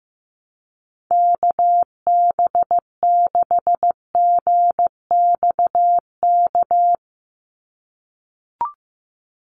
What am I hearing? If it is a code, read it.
Morse code, heard as a keyed tone, about 15 wpm, 700 Hz. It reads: KB6GXK
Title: USA Call Signs 15wpm